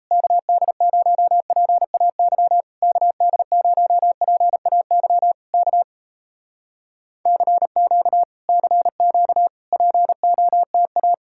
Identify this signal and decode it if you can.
Morse code, heard as a keyed tone, about 39 wpm, 700 Hz. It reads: KD0PAY KD0PAY K CQ CQ POTA